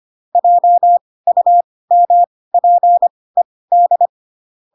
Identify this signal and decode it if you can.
Morse code, heard as a keyed tone, about 25 wpm, spaced as for 18 wpm, 700 Hz. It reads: JUMPED